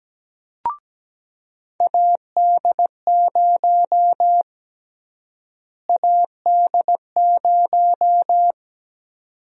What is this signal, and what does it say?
Morse code, heard as a keyed tone, about 17 wpm, 700 Hz. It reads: AD0 AD0
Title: USA Call Sign Prefixes - Repeated 2x - 17wpm